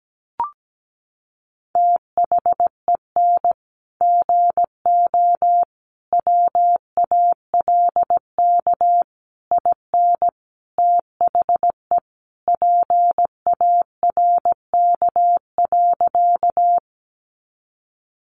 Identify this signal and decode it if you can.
Morse code, heard as a keyed tone, about 17 wpm, 700 Hz. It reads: THEN GO WALK IN THE PARK.
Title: Sentences from Top 1300 Words 17wpm